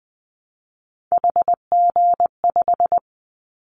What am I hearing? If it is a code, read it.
Morse code, heard as a keyed tone, about 20 wpm, 700 Hz. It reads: HG5
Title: Callsign Prefixes for Top 25 DXCC - Repeated 3x - 20wpm